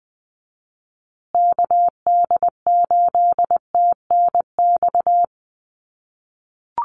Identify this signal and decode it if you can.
Morse code, heard as a keyed tone, about 20 wpm, 700 Hz. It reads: KD8TNX
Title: USA Call Signs Repeated 3x 20wpm